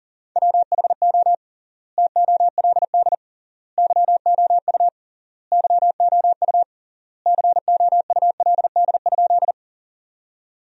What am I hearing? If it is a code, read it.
Morse code, heard as a keyed tone, about 37 wpm, 700 Hz. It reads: WHO TOLD YOU YOU COULD?
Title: 3x Word-Spacing - Sentences from Top 400 Words - 40wpm